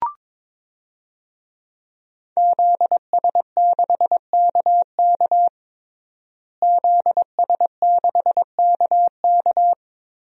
Morse code, heard as a keyed tone, about 21 wpm, 700 Hz. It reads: ZS6KK ZS6KK
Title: DXCC-100 Call Signs - 5 Characters - Repeated 3x - 22wpm